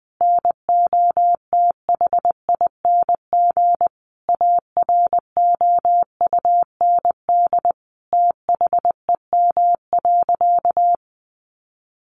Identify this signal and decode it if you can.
Morse code, heard as a keyed tone, about 20 wpm, 700 Hz. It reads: NOTHING AROUND THEM.
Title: Sentences from Top 600 Words 20wpm